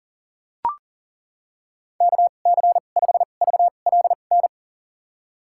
Morse code, heard as a keyed tone, about 40 wpm, spaced as for 29 wpm, 700 Hz. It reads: KC5VLN
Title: USA Call Signs - 6 Characters - 2x Character Spacing - 40wpm